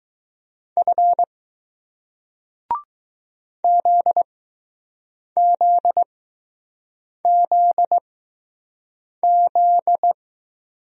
Morse code, heard as a keyed tone, about 20 wpm, spaced as for 6 wpm, 700 Hz. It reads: F ZZZZ